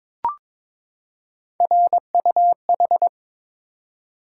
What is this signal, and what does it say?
Morse code, heard as a keyed tone, about 22 wpm, 700 Hz. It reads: RUH
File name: practice / 22wpm / Three Letter-Number Combinations - 22wpm